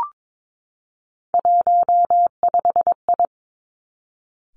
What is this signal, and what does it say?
Morse code, heard as a keyed tone, about 22 wpm, 700 Hz. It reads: 15I
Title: Three Challenging Letter-Number Combinations 22wpm